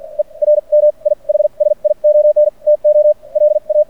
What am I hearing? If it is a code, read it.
Morse code, heard as a keyed tone, about 55 wpm, spaced as for 39 wpm, 590 Hz. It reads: ERMIHSION TO LA
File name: archived / CRSnet